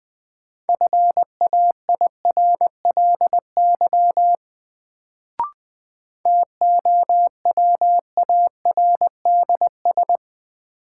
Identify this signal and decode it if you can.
Morse code, heard as a keyed tone, about 20 wpm, 700 Hz. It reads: FAIRLY TOWARDS
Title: Sentences from Top 1100 Words - New Word Drill Alt - 20wpm